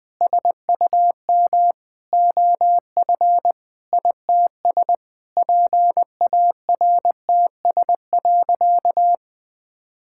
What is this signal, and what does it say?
Morse code, heard as a keyed tone, about 20 wpm, 700 Hz. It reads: SUM OF ITS PARTS.
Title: Sentences with 8-Word Phrases - 20wpm